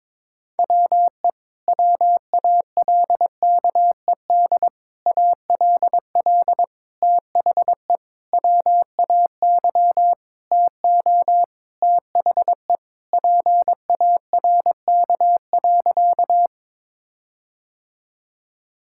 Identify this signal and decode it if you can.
Morse code, heard as a keyed tone, about 22 wpm, 700 Hz. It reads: WE WALKED ALL THE WAY TO THE PARK.
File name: practice / 22wpm / Sentences with 4-Word Phrases - 22wpm